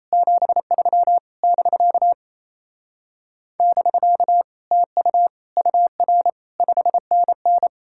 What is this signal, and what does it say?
Morse code, heard as a keyed tone, about 38 wpm, 700 Hz. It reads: 73 <BK> <BK> TU UR 5NN